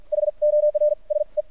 Morse code, heard as a keyed tone, about 53 wpm, spaced as for 40 wpm, 600 Hz. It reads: HORSE
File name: archived / CRSnet